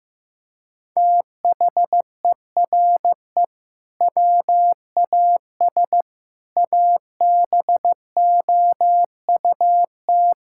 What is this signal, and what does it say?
Morse code, heard as a keyed tone, about 16 wpm, 700 Hz. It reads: THERE WAS ABOUT
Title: Sentences from Top 1000 Words - Review Part 5 - 15wpm